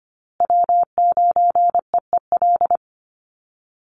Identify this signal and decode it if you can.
Morse code, heard as a keyed tone, about 25 wpm, 700 Hz. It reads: W9EEL
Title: USA Call Signs Repeated 2x 25wpm